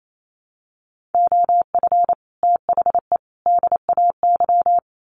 Morse code, heard as a keyed tone, about 28 wpm, 700 Hz. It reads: OF THE DAY
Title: Sets of 3 Words 28wpm